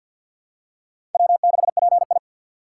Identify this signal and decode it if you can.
Morse code, heard as a keyed tone, about 47 wpm, 700 Hz. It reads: W6PI
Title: USA Call Signs - 4 Characters - 3x Character Spacing - 50wpm